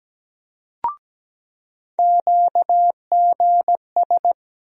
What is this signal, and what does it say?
Morse code, heard as a keyed tone, about 17 wpm, 700 Hz. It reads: QGS
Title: USA Call Sign Postfixes - 3 Letters 17wpm